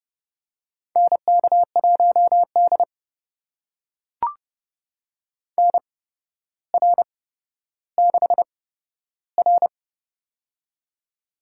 Morse code, heard as a keyed tone, about 31 wpm, spaced as for 13 wpm, 700 Hz. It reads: NK1D NR6R
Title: USA Call Signs - 8x Character Spacing - 30wpm